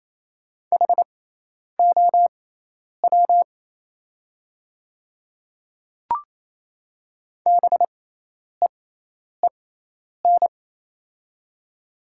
Morse code, heard as a keyed tone, about 28 wpm, spaced as for 10 wpm, 700 Hz. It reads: HOW BEEN
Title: 6x Character Spacing - Top 100 Words - 28wpm